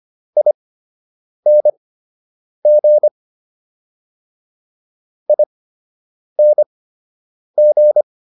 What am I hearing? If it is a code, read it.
Morse code, heard as a keyed tone, about 25 wpm, spaced as for 8 wpm, 600 Hz. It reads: ING ING